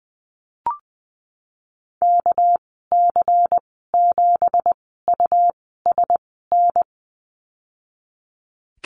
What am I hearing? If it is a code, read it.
Morse code, heard as a keyed tone, about 20 wpm, spaced as for 14 wpm, 700 Hz. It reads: KC7USN
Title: 245 Lesson 39g - Letter z - Call Signs 20wpm